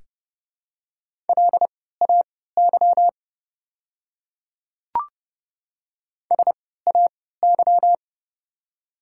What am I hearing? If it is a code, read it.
Morse code, heard as a keyed tone, about 30 wpm, spaced as for 17 wpm, 690 Hz. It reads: LAY SAY
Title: 3x Character Spacing - Top 500 Words - 30wpm